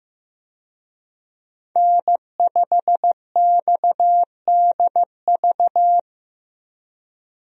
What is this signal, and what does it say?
Morse code, heard as a keyed tone, about 15 wpm, 700 Hz. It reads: N5XDV
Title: USA Call Signs - 5 Characters - Repeated 2x - 15wpm